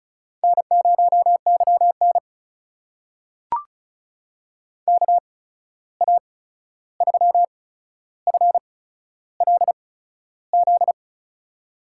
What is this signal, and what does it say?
Morse code, heard as a keyed tone, about 36 wpm, spaced as for 13 wpm, 700 Hz. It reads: N0YN KA3FLZ
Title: USA Call Signs - 8x Character Spacing - 35wpm